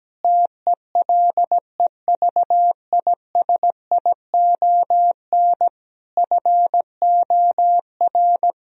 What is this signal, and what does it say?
Morse code, heard as a keyed tone, about 17 wpm, 700 Hz. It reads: TELEVISION FOR
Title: Sentences from Top 1000 Words - Review Part 4 - 17wpm